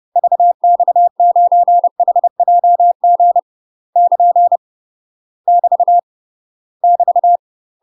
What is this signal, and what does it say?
Morse code, heard as a keyed tone, about 30 wpm, 700 Hz. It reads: VX9HJG <KN> <BT> <BT>
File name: UK MASTER QSOs TRAINING / 30WPM / UK MASTER QSOs 0046